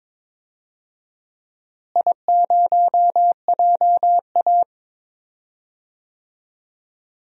Morse code, heard as a keyed tone, about 22 wpm, 700 Hz. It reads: I0JA
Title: DXCC-100 Call Signs - 4 Characters - Repeated 3x - 22wpm